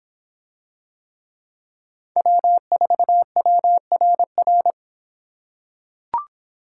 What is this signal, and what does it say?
Morse code, heard as a keyed tone, about 25 wpm, 700 Hz. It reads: W4WRR